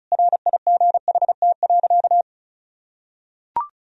Morse code, heard as a keyed tone, about 36 wpm, 700 Hz. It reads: RIGHT.
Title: Sentences from Top 1000 Words - Review Part 1 - 35wpm